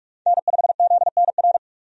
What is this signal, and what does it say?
Morse code, heard as a keyed tone, about 43 wpm, 700 Hz. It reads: N5ZNR
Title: USA Call Signs - 5 Characters - 45wpm